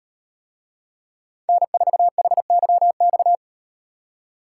Morse code, heard as a keyed tone, about 36 wpm, 700 Hz. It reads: N4HYX